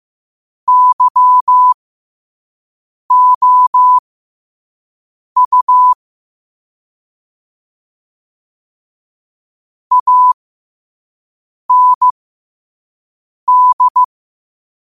Morse code, heard as a keyed tone, about 15 wpm, spaced as for 5 wpm, 1000 Hz. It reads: YOU AND
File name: qotdfiles / 5wpm